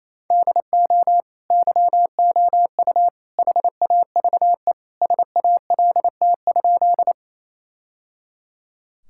Morse code, heard as a keyed tone, about 28 wpm, 700 Hz. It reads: DO YOU HAVE SALT?
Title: Sentences from Top 1000 Words - Review Part 5 - 28wpm